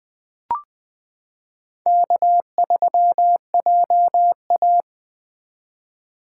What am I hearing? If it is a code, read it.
Morse code, heard as a keyed tone, about 20 wpm, 700 Hz. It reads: K3JA
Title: 2023 POTA Activator Call Signs - 20wpm